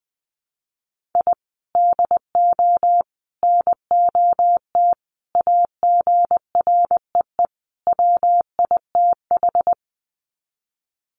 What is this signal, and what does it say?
Morse code, heard as a keyed tone, about 19 wpm, 700 Hz. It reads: I DO NOT AGREE WITH
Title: Sets of 5 Words Encore 20wpm